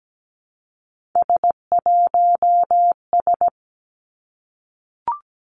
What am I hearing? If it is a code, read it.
Morse code, heard as a keyed tone, about 17 wpm, 700 Hz. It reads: S1S
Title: Three Challenging Letter-Number Combinations 17wpm